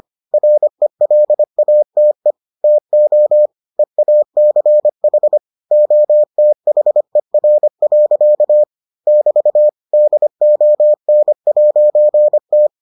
Morse code, heard as a keyed tone, about 25 wpm, 600 Hz. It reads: RELATE TO EACH OTHER. <BT> DON'T